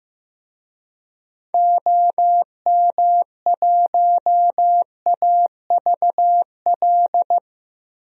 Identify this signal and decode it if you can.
Morse code, heard as a keyed tone, about 15 wpm, 700 Hz. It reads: OM1AVL
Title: DXCC-100 Call Signs - 6 Characters - 15wpm